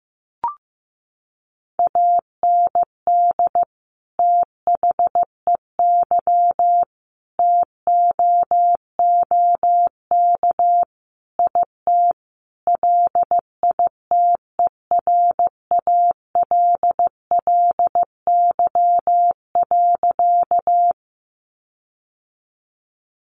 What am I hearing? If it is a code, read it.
Morse code, heard as a keyed tone, about 15 wpm, 700 Hz. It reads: AND THEY TOOK IT LITERALLY.
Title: Sentences from Top 1400 Words 15wpm